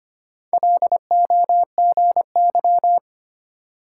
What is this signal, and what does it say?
Morse code, heard as a keyed tone, about 33 wpm, 700 Hz. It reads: LOGY